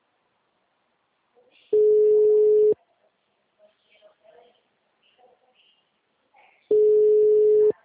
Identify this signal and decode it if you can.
Ringback tone (Europe, 425 Hz), with speech.
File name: out- -2009- - - .114